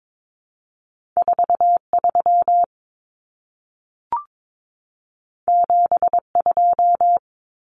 Morse code, heard as a keyed tone, about 22 wpm, 700 Hz. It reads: 43 72
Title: Common Cut Numbers - 2-Digit 22wpm